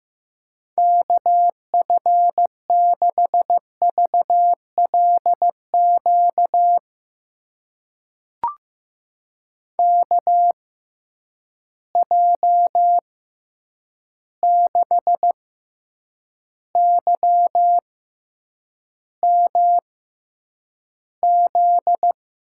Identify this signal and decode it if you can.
Morse code, heard as a keyed tone, about 15 wpm, spaced as for 8 wpm, 700 Hz. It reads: KF6VLQ KJ6YMZ
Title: USA Call Signs - 6 Characters - 6x Character Spacing - 15wpm